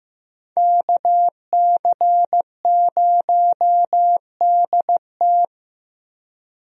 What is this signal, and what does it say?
Morse code, heard as a keyed tone, about 15 wpm, 700 Hz. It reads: KC0DT
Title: USA Call Signs - 5 Characters - Repeated 3x - 15wpm